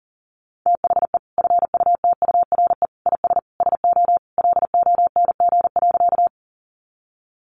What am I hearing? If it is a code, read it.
Morse code, heard as a keyed tone, about 40 wpm, 700 Hz. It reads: THE FUTURE IS SO LONG.
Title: Sentences from Top 700 Words 40wpm